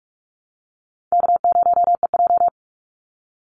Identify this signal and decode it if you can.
Morse code, heard as a keyed tone, about 44 wpm, 700 Hz. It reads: K0EJ
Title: Call Signs Repeated 2x - 45wpm